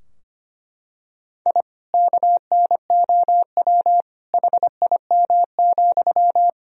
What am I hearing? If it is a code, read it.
Morse code, heard as a keyed tone, about 25 wpm, 700 Hz. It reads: I KNOW HIM,
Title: Sentences from Top 1000 Words - Review Part 4 - 25wpm